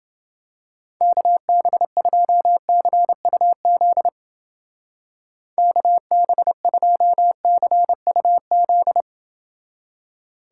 Morse code, heard as a keyed tone, about 29 wpm, 700 Hz. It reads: KB2CUZ KB2CUZ